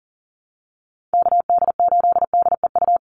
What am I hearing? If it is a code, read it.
Morse code, heard as a keyed tone, about 40 wpm, 700 Hz. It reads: KD8DEU
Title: NA SOTA Chaser Callsigns 40wpm